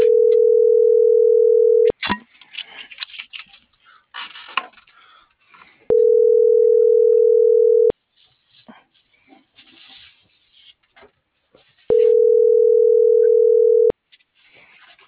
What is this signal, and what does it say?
Ringback tone (North America), 440 + 480 Hz.